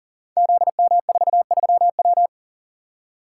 Morse code, heard as a keyed tone, about 40 wpm, 700 Hz. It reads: ZM43W
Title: Code Groups - 5 Letter:Numbers - 40wpm